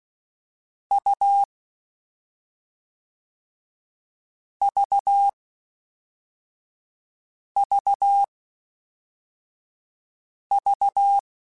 Morse code, heard as a keyed tone, about 16 wpm, 800 Hz. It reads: U V V V
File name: Difficult Combos / Difficult Combos-mp3s / AUV AUV4 @5